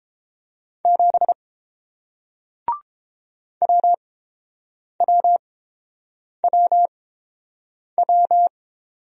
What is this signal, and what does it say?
Morse code, heard as a keyed tone, about 31 wpm, spaced as for 7 wpm, 700 Hz. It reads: 7 WWWW